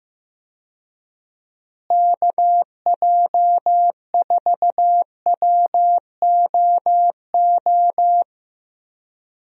Morse code, heard as a keyed tone, about 15 wpm, 700 Hz. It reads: KJ4WOO
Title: USA Call Signs - 6 Characters - Repeated 3x - 15wpm